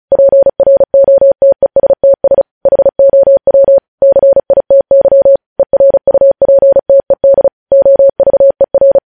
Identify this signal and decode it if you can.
Morse code, heard as a keyed tone, about 36 wpm, 550 Hz. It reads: PROTESTS HOW CITY ERUPTED OVER
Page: news headlines in morse code at 35 wpm